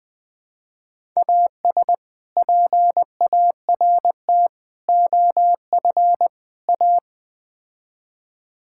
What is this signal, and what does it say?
Morse code, heard as a keyed tone, about 20 wpm, 700 Hz. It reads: AS PART OF A
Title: Build A Phrase - 4-Word Phrases - 20wpm